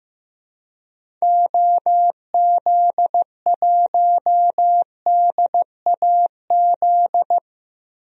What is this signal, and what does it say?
Morse code, heard as a keyed tone, about 15 wpm, 700 Hz. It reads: OZ1DAZ
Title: DXCC-100 Call Signs - 6 Characters - 15wpm